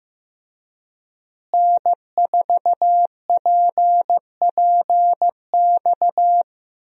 Morse code, heard as a keyed tone, about 16 wpm, 700 Hz. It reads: N4PPX
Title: USA Call Signs - 5 Characters - Repeated 2x - 15wpm